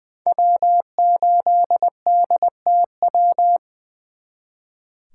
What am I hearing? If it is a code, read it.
Morse code, heard as a keyed tone, about 24 wpm, 700 Hz. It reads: W8DTW
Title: Speed Racing - USA 1x3 Call Signs 20wpm